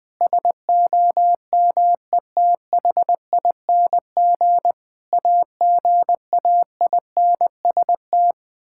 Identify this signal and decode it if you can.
Morse code, heard as a keyed tone, about 21 wpm, 700 Hz. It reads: SOMETHING AGAINST
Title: Sentences with 9-Word Phrases - 20wpm